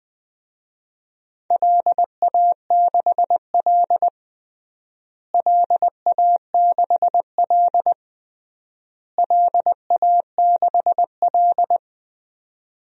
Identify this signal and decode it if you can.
Morse code, heard as a keyed tone, about 21 wpm, 700 Hz. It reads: LA6L LA6L LA6L
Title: DXCC-100 Call Signs - 4 Characters - Repeated 3x - 20wpm